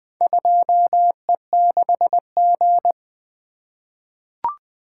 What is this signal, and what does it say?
Morse code, heard as a keyed tone, about 20 wpm, 700 Hz. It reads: 2E6G
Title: Four Letter-Number Combinations - 20wpm